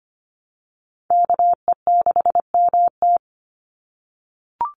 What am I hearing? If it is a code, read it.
Morse code, heard as a keyed tone, about 25 wpm, 700 Hz. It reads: KE6MT
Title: NA SOTA Chaser Callsigns 25wpm